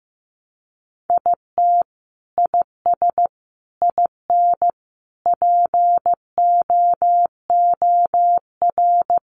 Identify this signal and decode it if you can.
Morse code, heard as a keyed tone, about 15 wpm, 700 Hz. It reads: IT IS IN POOR
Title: Sentences from Top 1200 Words 15wpm